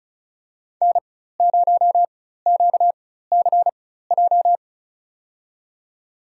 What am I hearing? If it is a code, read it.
Morse code, heard as a keyed tone, about 36 wpm, spaced as for 16 wpm, 700 Hz. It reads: N0QCJ
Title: USA Call Signs - 5 Characters - 4x Character Spacing - 35wpm